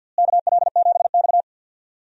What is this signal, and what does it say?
Morse code, heard as a keyed tone, about 53 wpm, 700 Hz. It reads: KL7X
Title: DXCC-100 Call Signs - 4 Characters - Repeated 2x - 50wpm